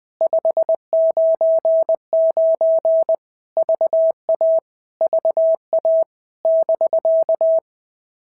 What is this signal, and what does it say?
Morse code, heard as a keyed tone, about 20 wpm, 650 Hz. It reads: 599 VA VA <BK>
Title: POTA QSO Practice - 20wpm